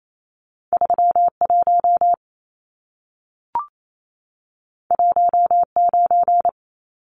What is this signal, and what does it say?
Morse code, heard as a keyed tone, about 28 wpm, 700 Hz. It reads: 31 19
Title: Numbers - 2-Digit 28wpm